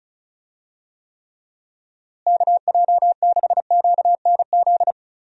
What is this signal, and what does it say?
Morse code, heard as a keyed tone, about 36 wpm, 700 Hz. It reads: KJ6QNZ
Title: USA Call Signs - 6 Characters - Repeated 2x - 35wpm